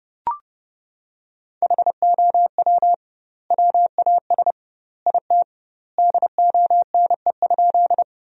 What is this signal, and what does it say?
Morse code, heard as a keyed tone, about 30 wpm, 700 Hz. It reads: HOW WAS IT DONE?
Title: 2x Word-Spacing - Sentences from Top 500 Words - 30wpm